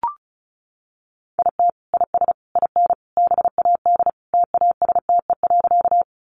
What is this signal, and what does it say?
Morse code, heard as a keyed tone, about 34 wpm, 700 Hz. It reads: IT IS IN BAD TASTE.
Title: Sentences from Top 1100 Words 35wpm